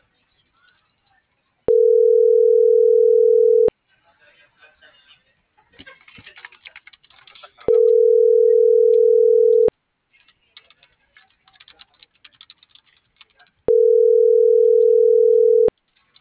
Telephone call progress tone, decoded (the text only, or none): ringback tone